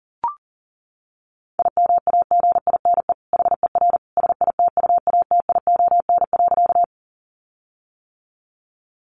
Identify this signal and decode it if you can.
Morse code, heard as a keyed tone, about 40 wpm, 700 Hz. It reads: IMAGINE HER SITUATION.
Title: Sentences from Top 1400 Words 40wpm